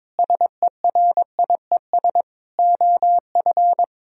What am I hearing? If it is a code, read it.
Morse code, heard as a keyed tone, about 22 wpm, 700 Hz. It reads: SERIES OF
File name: practice / 22wpm / Sentences with 8-Word Phrases - 22wpm